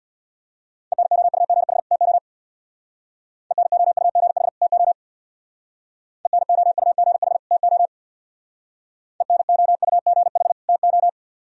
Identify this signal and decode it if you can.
Morse code, heard as a keyed tone, about 64 wpm, 690 Hz. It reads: ENOUGH TO ENOUGH TO ENOUGH TO ENOUGH TO